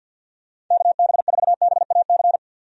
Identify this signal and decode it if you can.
Morse code, heard as a keyed tone, about 50 wpm, 700 Hz. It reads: KB4BAC
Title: USA Call Signs - 6 Characters - 50wpm